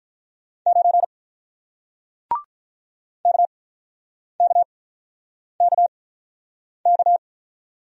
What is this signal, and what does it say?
Morse code, heard as a keyed tone, about 48 wpm, spaced as for 9 wpm, 700 Hz. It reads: 9 KKKK